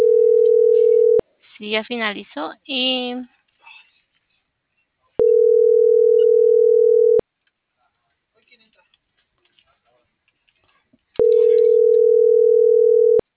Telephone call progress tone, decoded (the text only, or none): ringback tone